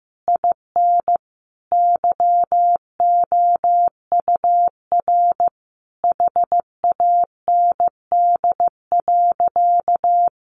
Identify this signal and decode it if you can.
Morse code, heard as a keyed tone, about 15 wpm, 700 Hz. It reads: IN YOUR HAND.
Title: Sentences from Top 1500 Words 15wpm